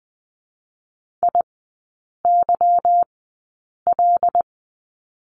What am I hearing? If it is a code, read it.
Morse code, heard as a keyed tone, about 20 wpm, spaced as for 8 wpm, 690 Hz. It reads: IYL
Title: 205 Lesson 33e - Letter j - Three Characters 20wpm